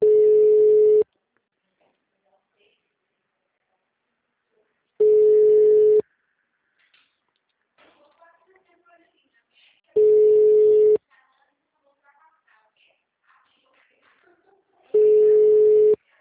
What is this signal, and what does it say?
Ringback tone (Europe, 425 Hz), with speech.